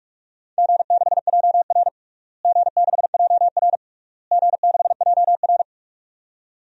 Morse code, heard as a keyed tone, about 42 wpm, 700 Hz. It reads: G6JR G6JR G6JR
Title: DXCC-100 Call Signs - 4 Characters - Repeated 3x - 45wpm